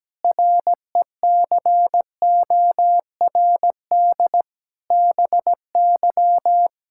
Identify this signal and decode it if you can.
Morse code, heard as a keyed tone, about 17 wpm, 700 Hz. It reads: RECORD BY
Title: Pangram Challenge v2 - 17wpm